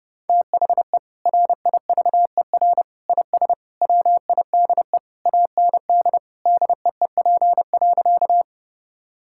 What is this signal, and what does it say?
Morse code, heard as a keyed tone, about 30 wpm, 700 Hz. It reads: THE RIVER IS WIDE AND DEEP.
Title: Sentences from Top 1000 Words - Review Part 6 - 30wpm